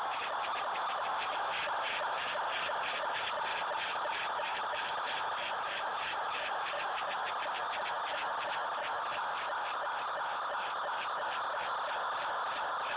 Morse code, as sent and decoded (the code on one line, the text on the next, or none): none